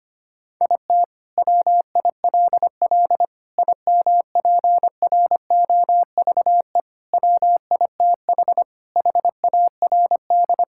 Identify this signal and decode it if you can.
Morse code, heard as a keyed tone, about 25 wpm, 700 Hz. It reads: IT WILL IMPROVE WITH HARD
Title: Sentences from Top 1000 Words - Review Part 5 - 25wpm